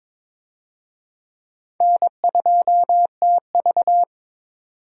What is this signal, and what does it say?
Morse code, heard as a keyed tone, about 22 wpm, 700 Hz. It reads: N2TV
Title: USA Call Signs - 4 Characters - 22wpm